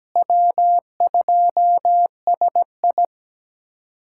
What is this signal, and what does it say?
Morse code, heard as a keyed tone, about 17 wpm, 700 Hz. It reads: W2SI
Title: USA Callsign DE Callsign - 4 Characters - 17wpm